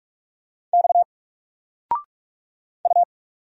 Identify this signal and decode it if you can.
Morse code, heard as a keyed tone, about 44 wpm, 700 Hz. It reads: X U
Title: Single Letter-Number Alt 45wpm